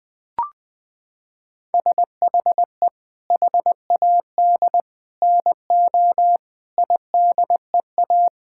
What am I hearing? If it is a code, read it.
Morse code, heard as a keyed tone, about 20 wpm, 700 Hz. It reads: SHE HAD NO IDEA
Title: Sentences from Top 1000 Words - Review Part 6 - 20wpm